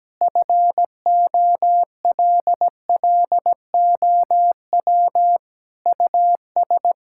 Morse code, heard as a keyed tone, about 17 wpm, 700 Hz. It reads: FOLLOW US
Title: Sentences from Top 1600 Words 17wpm